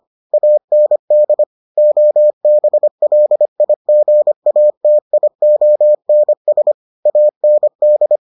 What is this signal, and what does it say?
Morse code, heard as a keyed tone, about 25 wpm, 600 Hz. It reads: AND OBLIGATIONS AND